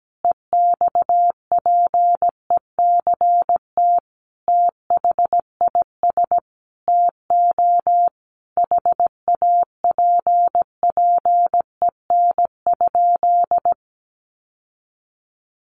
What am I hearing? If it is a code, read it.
Morse code, heard as a keyed tone, about 17 wpm, 700 Hz. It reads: EXPECT THIS TO HAPPEN?
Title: Sentences from Top 1100 Words 17wpm